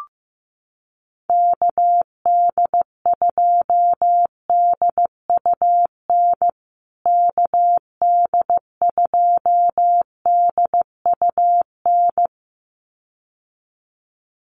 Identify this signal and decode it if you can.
Morse code, heard as a keyed tone, about 15 wpm, 700 Hz. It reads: KD2DUN KD2DUN
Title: USA Call Signs Repeated 2x 15wpm